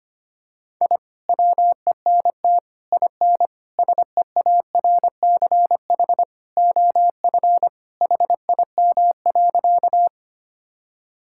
Morse code, heard as a keyed tone, about 25 wpm, 700 Hz. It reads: I WENT IN SEARCH OF HIM.
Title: Sentences from Top 1000 Words - Review Part 5 - 25wpm